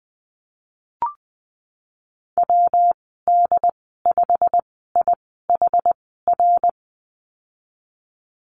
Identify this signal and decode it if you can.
Morse code, heard as a keyed tone, about 20 wpm, spaced as for 15 wpm, 700 Hz. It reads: WD5IHR
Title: 179 Lesson 29g - Slash - Call Signs 20wpm